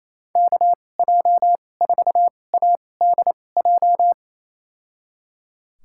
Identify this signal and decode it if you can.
Morse code, heard as a keyed tone, about 28 wpm, spaced as for 20 wpm, 700 Hz. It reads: KJ4ADJ
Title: USA Call Signs - 6 Characters - 2x Character Spacing - 28wpm